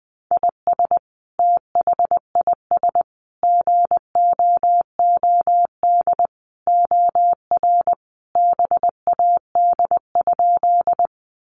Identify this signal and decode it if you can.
Morse code, heard as a keyed tone, about 20 wpm, 700 Hz. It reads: IS THIS GOOD OR BAD?
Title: Sentences from Top 700 Words 20wpm